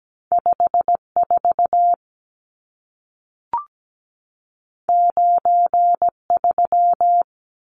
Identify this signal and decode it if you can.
Morse code, heard as a keyed tone, about 17 wpm, 700 Hz. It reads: 54 93
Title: Numbers - 2-Digit 17wpm